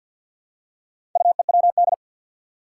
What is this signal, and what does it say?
Morse code, heard as a keyed tone, about 53 wpm, 700 Hz. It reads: UEWD
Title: Four Challenging Letter-Number Combinations 50wpm